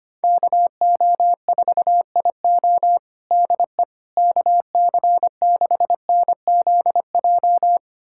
Morse code, heard as a keyed tone, about 25 wpm, 700 Hz. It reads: KO4IO DE KC6NZJ
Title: USA Callsign DE Callsign 25wpm